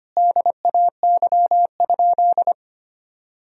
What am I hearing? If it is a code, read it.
Morse code, heard as a keyed tone, about 25 wpm, 700 Hz. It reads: DAY?
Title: Sentences from Top 1000 Words - Review Part 2 - 25wpm